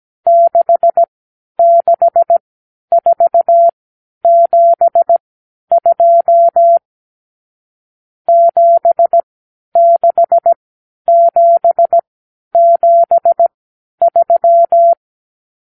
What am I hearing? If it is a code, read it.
Morse code, heard as a keyed tone, about 17 wpm, spaced as for 10 wpm, 680 Hz. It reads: 66472 76773
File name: k7qo / Lesson 52 (8)